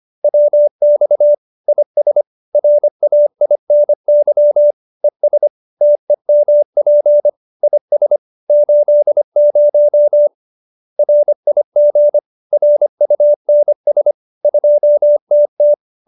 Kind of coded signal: Morse code

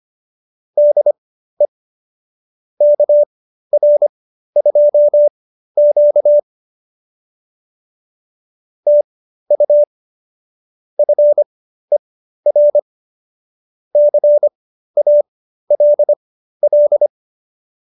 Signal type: Morse code

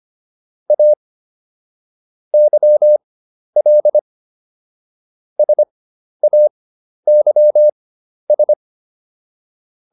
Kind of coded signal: Morse code

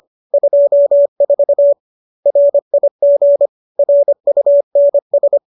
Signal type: Morse code

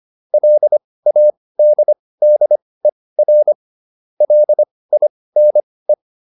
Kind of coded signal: Morse code

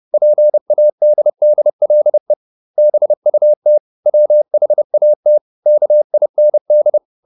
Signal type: Morse code

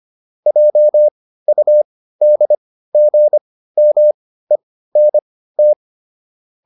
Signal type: Morse code

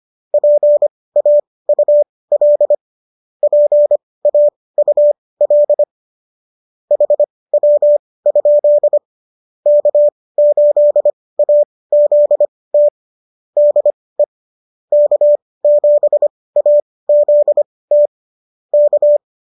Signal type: Morse code

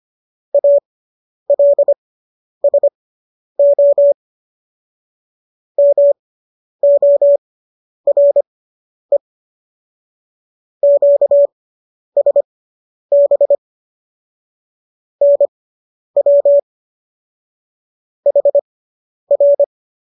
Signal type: Morse code

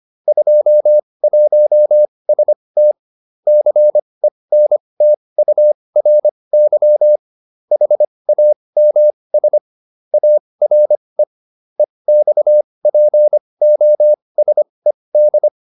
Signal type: Morse code

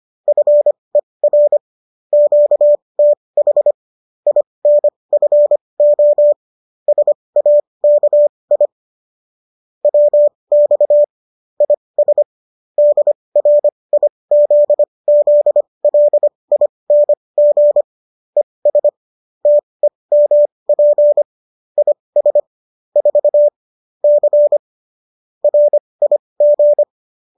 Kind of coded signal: Morse code

